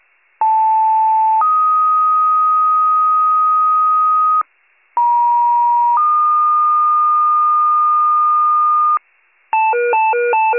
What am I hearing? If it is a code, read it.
Two-tone page, heard as a one-second tone, then a three-second tone, about 880 then 1250 Hz, about 950 then 1220 Hz. It is heard from 0.4 to 9.0 s.